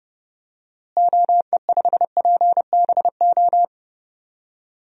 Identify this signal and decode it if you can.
Morse code, heard as a keyed tone, about 30 wpm, 700 Hz. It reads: OE5PBO
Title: DXCC-100 Call Signs - 6 Characters - Repeated 2x - 30wpm